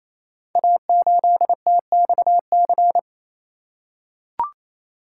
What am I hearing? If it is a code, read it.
Morse code, heard as a keyed tone, about 27 wpm, 700 Hz. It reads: A8TXC
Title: Code Groups - 5 Letter:Numbers - 28wpm